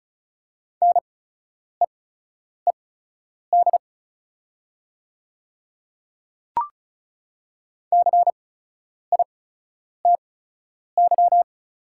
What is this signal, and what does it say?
Morse code, heard as a keyed tone, about 36 wpm, spaced as for 10 wpm, 700 Hz. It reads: NEED CITY
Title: 8x Character Spacing - Top 300 Words - 35wpm